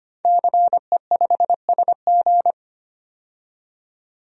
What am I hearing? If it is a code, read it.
Morse code, heard as a keyed tone, about 25 wpm, 700 Hz. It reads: CE5SG
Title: DXCC-100 Call Signs - 5 Characters - 25wpm